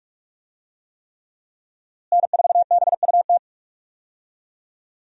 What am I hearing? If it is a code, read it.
Morse code, heard as a keyed tone, about 44 wpm, 700 Hz. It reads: N4BUT
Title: USA Call Signs - 5 Characters - Repeated 3x - 45wpm